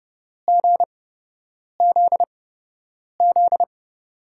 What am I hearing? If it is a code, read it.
Morse code, heard as a keyed tone, about 31 wpm, spaced as for 8 wpm, 700 Hz. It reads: GZZ
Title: USA Call Signs - 5 Characters - 8x Character Spacing - 30wpm